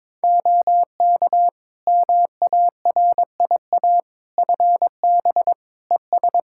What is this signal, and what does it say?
Morse code, heard as a keyed tone, about 22 wpm, 700 Hz. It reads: OK MARIA FB ES
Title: Practice QSO - Exchange 2 - Info - 22wpm